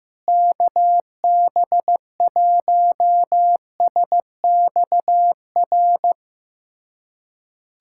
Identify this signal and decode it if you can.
Morse code, heard as a keyed tone, about 15 wpm, 700 Hz. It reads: KB1SXR
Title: USA Callsign DE Callsign - 6 Characters - 15wpm